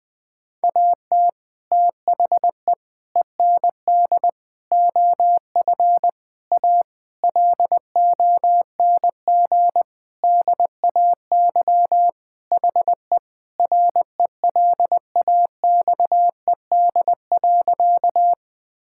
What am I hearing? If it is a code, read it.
Morse code, heard as a keyed tone, about 20 wpm, 700 Hz. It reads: AT THE END OF A LONG DAY HE RELAXED.
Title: Sentences with 5-Word Phrases - Spoken Beginning - 20wpm